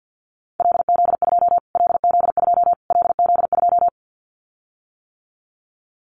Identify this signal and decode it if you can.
Morse code, heard as a keyed tone, about 50 wpm, 700 Hz. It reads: LZ2 LZ2 LZ2
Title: Callsign Prefixes for Top 25 DXCC - Repeated 3x - 50wpm